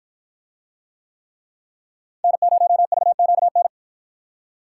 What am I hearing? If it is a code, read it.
Morse code, heard as a keyed tone, about 49 wpm, 700 Hz. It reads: N0VQN